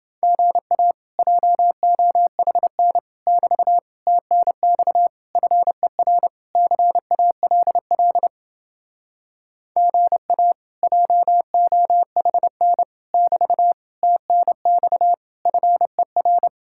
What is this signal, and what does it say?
Morse code, heard as a keyed tone, about 29 wpm, 700 Hz. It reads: GA JOHN <BT> TNX FER CALL GA JOHN <BT> TNX FER